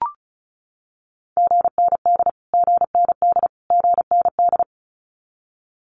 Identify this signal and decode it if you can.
Morse code, heard as a keyed tone, about 34 wpm, 700 Hz. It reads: GND GND GND
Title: Abbreviations Repeated 3x - 35wpm